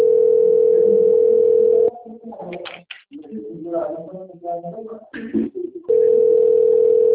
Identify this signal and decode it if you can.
Ringback tone (North America), 440 + 480 Hz.
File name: out- - - - - .27426